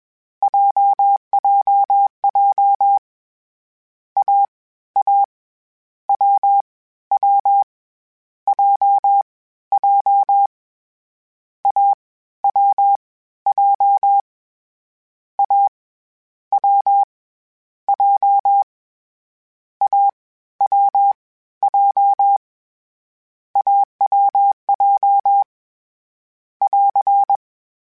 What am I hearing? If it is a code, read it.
Morse code, heard as a keyed tone, about 21 wpm, 800 Hz. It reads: JJJ A A W W J J A W J A W J A W J AWJ <AR>
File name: Difficult Combos / Difficult Combos-mp3s / AWJ1 @20